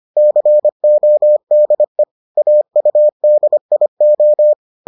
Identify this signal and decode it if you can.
Morse code, heard as a keyed tone, about 25 wpm, 600 Hz. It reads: CODE AUDIO